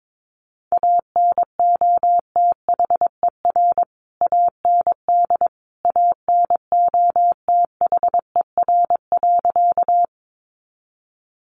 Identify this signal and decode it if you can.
Morse code, heard as a keyed tone, about 22 wpm, 700 Hz. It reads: ANOTHER AND ANOTHER.
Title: Sentences from Top 600 Words 22wpm